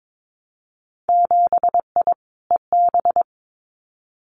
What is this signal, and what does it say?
Morse code, heard as a keyed tone, about 22 wpm, 700 Hz. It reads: 7I EB
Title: Winter Field Day Exchanges Encore 22wpm